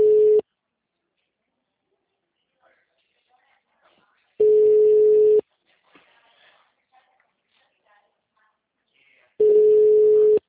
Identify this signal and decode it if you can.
Ringback tone (Europe, 425 Hz), with speech.
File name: out- -2033- - - .1716